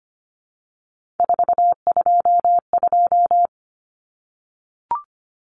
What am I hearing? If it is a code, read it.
Morse code, heard as a keyed tone, about 25 wpm, 700 Hz. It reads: 422
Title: Common Cut Numbers - 3-Digit 25wpm